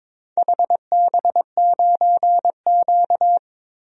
Morse code, heard as a keyed tone, about 22 wpm, 700 Hz. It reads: HB9Q
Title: DXCC-100 Call Signs - 4 Characters - 22wpm